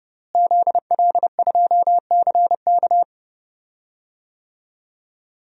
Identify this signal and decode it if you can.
Morse code, heard as a keyed tone, about 31 wpm, 700 Hz. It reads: ZL2CK
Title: DXCC-100 Call Signs - 5 Characters - Repeated 3x - 30wpm